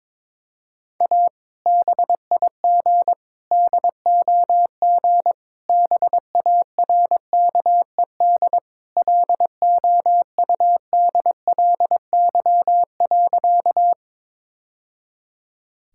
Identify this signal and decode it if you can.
Morse code, heard as a keyed tone, about 22 wpm, 700 Hz. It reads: A BIG DOG BARKED LOUDLY.
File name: practice / 22wpm / Sentences with 2-Word Phrases - 22wpm